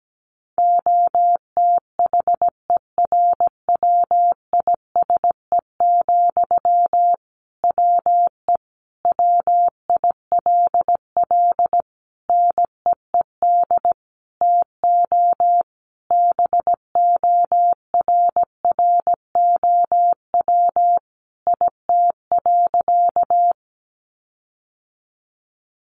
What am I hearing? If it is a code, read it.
Morse code, heard as a keyed tone, about 17 wpm, 700 Hz. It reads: OTHERWISE, WE WILL NEED TO BORROW IT.
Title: Sentences from Top 1500 Words 17wpm